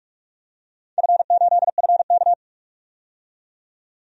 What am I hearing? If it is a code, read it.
Morse code, heard as a keyed tone, about 41 wpm, 700 Hz. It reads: F8FK
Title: DXCC-100 Call Signs - 4 Characters - 45wpm